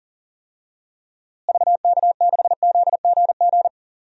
Morse code, heard as a keyed tone, about 40 wpm, 700 Hz. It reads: VK6ZGG
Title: DXCC-100 Call Signs - 6 Characters - Repeated 2x - 40wpm